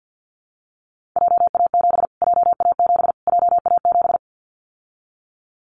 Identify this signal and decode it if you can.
Morse code, heard as a keyed tone, about 49 wpm, 700 Hz. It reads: JA7 JA7 JA7
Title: Callsign Prefixes for Top 25 DXCC - Repeated 3x - 50wpm